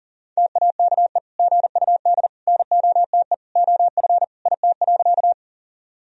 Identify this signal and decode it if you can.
Morse code, heard as a keyed tone, about 40 wpm, 700 Hz. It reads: TAKE GUD NOTE OF IT.
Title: Sentences from Top 500 Words and Abbreviations 40wpm